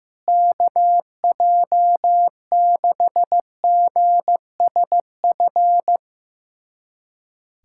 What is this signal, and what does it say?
Morse code, heard as a keyed tone, about 15 wpm, 700 Hz. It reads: KJ6GSF
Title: USA Callsign DE Callsign - 6 Characters - 15wpm